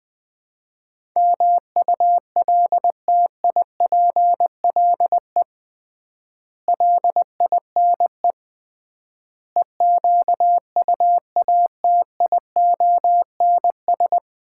Morse code, heard as a keyed tone, about 21 wpm, 700 Hz. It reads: MULTIPLE LINE EQUATIONS
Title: 3x Word-Spacing - Sets of 5 Words - 20wpm